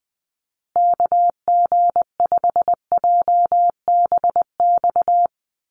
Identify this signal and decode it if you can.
Morse code, heard as a keyed tone, about 20 wpm, 700 Hz. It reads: KG5JBX
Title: USA Call Signs Repeated 3x 20wpm